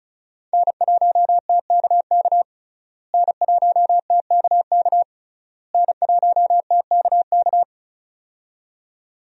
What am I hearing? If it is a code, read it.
Morse code, heard as a keyed tone, about 36 wpm, 700 Hz. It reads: N1TKK N1TKK N1TKK
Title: USA Call Signs - 5 Characters - Repeated 3x - 35wpm